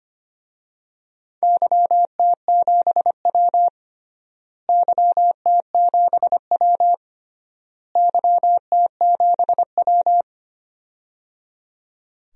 Morse code, heard as a keyed tone, about 25 wpm, 700 Hz. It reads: YT7W YT7W YT7W
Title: DXCC-100 Call Signs - 4 Characters - Repeated 3x - 25wpm